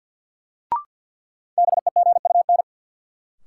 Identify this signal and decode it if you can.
Morse code, heard as a keyed tone, about 52 wpm, 700 Hz. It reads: BEGUN
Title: Sentences from Top 1000 Words - Review Part 6 - New Word Drill - 50wpm